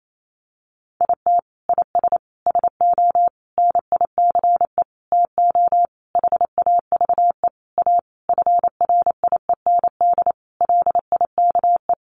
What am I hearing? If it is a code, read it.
Morse code, heard as a keyed tone, about 28 wpm, 700 Hz. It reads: IT IS SO NICE TO HAVE A FRIEND LIKE
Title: Sentences from Top 1100 Words 28wpm